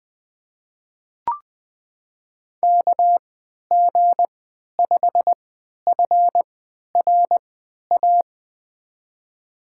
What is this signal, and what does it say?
Morse code, heard as a keyed tone, about 20 wpm, spaced as for 11 wpm, 700 Hz. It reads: KG5FRA
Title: USA Call Signs - 3x Character Spacing - 20wpm